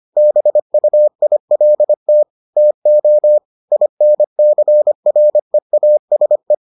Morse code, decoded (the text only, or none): BUILT TO INCREASE